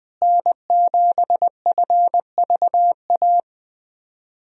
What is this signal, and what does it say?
Morse code, heard as a keyed tone, about 20 wpm, 700 Hz. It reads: N7FVA
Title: USA Call Signs - 5 Characters - Repeated 2x - 20wpm